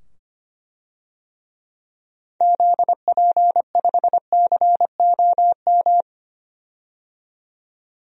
Morse code, heard as a keyed tone, about 25 wpm, 700 Hz. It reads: ZP5COM
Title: DXCC-100 Call Signs - 6 Characters - 25wpm